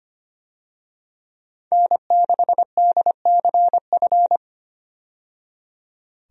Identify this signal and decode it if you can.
Morse code, heard as a keyed tone, about 25 wpm, 700 Hz. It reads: N6DCF
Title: USA Call Signs - 5 Characters - 25wpm